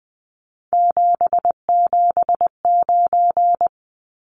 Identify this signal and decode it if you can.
Morse code, heard as a keyed tone, about 20 wpm, 700 Hz. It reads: 779
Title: Numbers - 3-Digit 20wpm